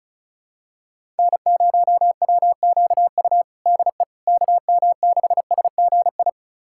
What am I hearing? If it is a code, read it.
Morse code, heard as a keyed tone, about 34 wpm, 700 Hz. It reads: N0WQU DE KM6SGI
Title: USA Callsign DE Callsign 35wpm